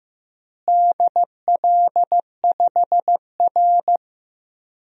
Morse code, heard as a keyed tone, about 15 wpm, 700 Hz. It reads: DL5R
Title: DXCC-100 Call Signs - 4 Characters - Repeated 3x - 15wpm